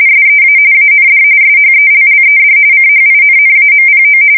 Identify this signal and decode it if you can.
RTTY, about 45.45 baud. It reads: SEA AND SHELLED CHINNAMPO
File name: fsk170 / USS-New-Jersey